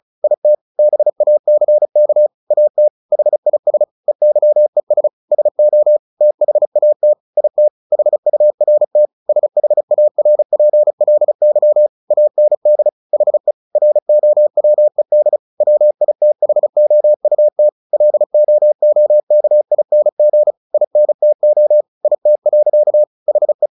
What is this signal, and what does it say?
Morse code, heard as a keyed tone, about 34 wpm, 600 Hz. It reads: IT BACK AT HIS EYES SO THAT IT HURT SHARPLY AND HE ROWED WITHOUT LOOKING INTO IT. HE
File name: the old man and the sea / the old man and the sea 1 35wpm